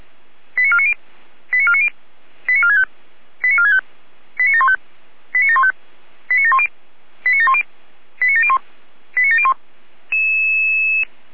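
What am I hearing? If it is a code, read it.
Five-tone selective call (ZVEI), heard as quick runs of five tones, tones of about 70 ms. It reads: 89490 89490 89465 89465 89715 89715 89710 89710 89701 89701
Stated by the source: Five-tone ZVEI selective call 89533, received on SDR.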